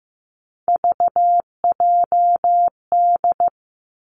Morse code, heard as a keyed tone, about 15 wpm, 700 Hz. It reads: VJD
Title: Three Challenging Letter-Number Combinations 15wpm